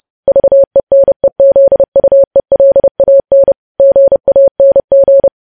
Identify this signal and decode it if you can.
Morse code, heard as a keyed tone, about 30 wpm, 550 Hz. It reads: VENEZUELAN GANG
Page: news headlines in morse code at 30 wpm